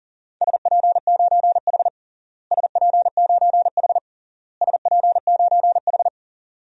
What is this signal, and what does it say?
Morse code, heard as a keyed tone, about 37 wpm, 700 Hz. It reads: SP9H SP9H SP9H
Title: DXCC-100 Call Signs - 4 Characters - Repeated 3x - 40wpm